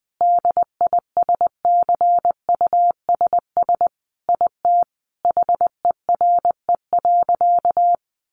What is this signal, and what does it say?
Morse code, heard as a keyed tone, about 19 wpm, 700 Hz. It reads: DISCUSS IT HERE.
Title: Sentences from Top 1100 Words 20wpm